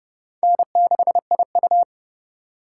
Morse code, heard as a keyed tone, about 26 wpm, 700 Hz. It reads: N6IU